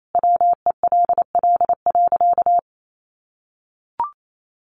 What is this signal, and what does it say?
Morse code, heard as a keyed tone, about 28 wpm, 700 Hz. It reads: WELL.
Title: Sentences from Top 800 Words 28wpm